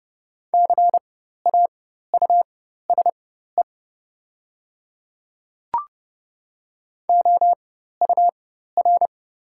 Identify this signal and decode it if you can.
Morse code, heard as a keyed tone, about 30 wpm, spaced as for 14 wpm, 700 Hz. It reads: CAUSE OUR
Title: 4x Character Spacing - Top 300 Words - 30wpm